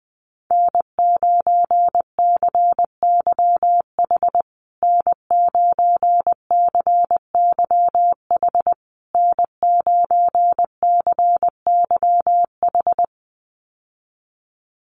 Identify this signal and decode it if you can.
Morse code, heard as a keyed tone, about 20 wpm, 700 Hz. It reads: N9CYH N9CYH N9CYH
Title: USA Call Signs Repeated 3x 20wpm